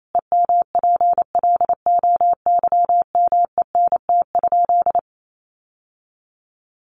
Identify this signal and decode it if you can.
Morse code, heard as a keyed tone, about 28 wpm, 700 Hz. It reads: EMPLOYMENT?
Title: Sentences from Top 1500 Words 28wpm